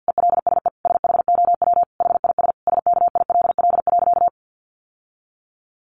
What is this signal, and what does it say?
Morse code, heard as a keyed tone, about 50 wpm, 700 Hz. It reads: ELSE SHOW HIS SKILL.
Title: Sentences from Top 1200 Words 50wpm